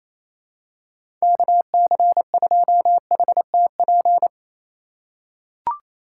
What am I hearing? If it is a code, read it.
Morse code, heard as a keyed tone, about 28 wpm, 700 Hz. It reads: KC2HTP
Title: USA Call Signs - 6 Characters - 2x Character Spacing - 28wpm